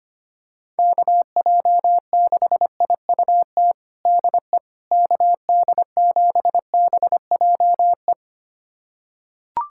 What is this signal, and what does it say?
Morse code, heard as a keyed tone, about 25 wpm, 700 Hz. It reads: KJ6IUT DE KD7BJE
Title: USA Callsign DE Callsign 25wpm